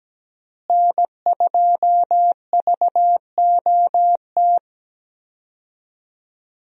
Morse code, heard as a keyed tone, about 17 wpm, 700 Hz. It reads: N2VOT
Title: USA Call Signs - 5 Characters - Repeated 3x - 17wpm